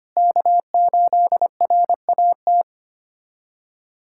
Morse code, heard as a keyed tone, about 25 wpm, 700 Hz. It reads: K8RAT
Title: USA Callsign DE Callsign 25wpm